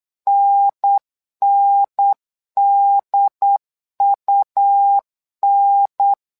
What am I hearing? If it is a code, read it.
Morse code, heard as a keyed tone, about 8 wpm, 800 Hz. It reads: NNDUN
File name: LEZIONI / 2 UHDN / 2 UHDN 8wpm2232